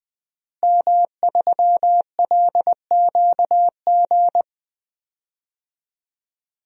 Morse code, heard as a keyed tone, about 20 wpm, 700 Hz. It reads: M3LQG
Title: DXCC-100 Call Signs - 5 Characters - Repeated 2x - 20wpm